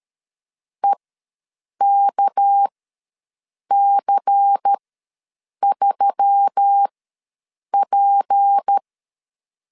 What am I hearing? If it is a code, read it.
Morse code, heard as a keyed tone, about 13 wpm, spaced as for 7 wpm, 800 Hz. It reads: EKC3P
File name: LEZIONI / 6 PKX0 / 6 PK ws 12wpm 3116